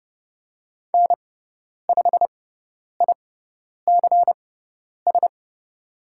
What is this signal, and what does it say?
Morse code, heard as a keyed tone, about 30 wpm, spaced as for 10 wpm, 700 Hz. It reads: N5ICS
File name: Lesson 17g - Letter u - Call Signs 30wpm